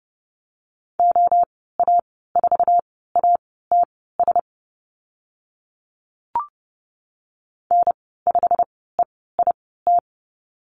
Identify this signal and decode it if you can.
Morse code, heard as a keyed tone, about 30 wpm, spaced as for 17 wpm, 700 Hz. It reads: OA4ATS N5EIT
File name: Lesson 15g - Number 5 - Call Signs 30wpm